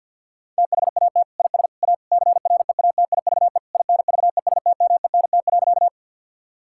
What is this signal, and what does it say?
Morse code, heard as a keyed tone, about 48 wpm, 700 Hz. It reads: THAT IS A CREATIVE INVESTMENT.
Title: Sentences from Top 1000 Words - Review Part 5 - 50wpm